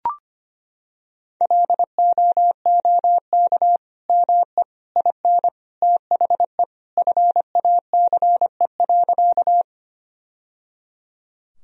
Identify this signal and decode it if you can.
Morse code, heard as a keyed tone, about 25 wpm, 700 Hz. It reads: LOOK ME IN THE FACE.
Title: Sentences from Top 1000 Words - Review Part 2 - 25wpm